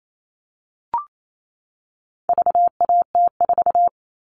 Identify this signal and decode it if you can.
Morse code, heard as a keyed tone, about 28 wpm, 700 Hz. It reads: VAT4
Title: All Cut Numbers - 4-Digit 28wpm